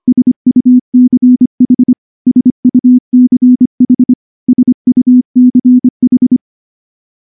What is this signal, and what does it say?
Morse code, heard as a keyed tone, about 23 wpm, 250 Hz. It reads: SUCH SUCH SUCH
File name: morse / morse-training-25WPM